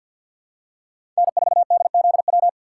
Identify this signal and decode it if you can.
Morse code, heard as a keyed tone, about 52 wpm, 700 Hz. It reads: N4DZW
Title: USA Call Signs - 5 Characters - 4x Character Spacing - 50wpm